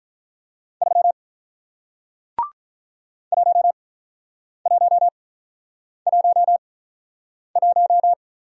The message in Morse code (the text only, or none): 3 1111